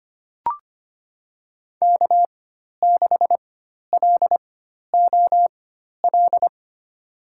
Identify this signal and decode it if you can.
Morse code, heard as a keyed tone, about 25 wpm, spaced as for 12 wpm, 700 Hz. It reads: K6LOL
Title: USA Call Signs - 5 Characters - 4x Character Spacing - 25wpm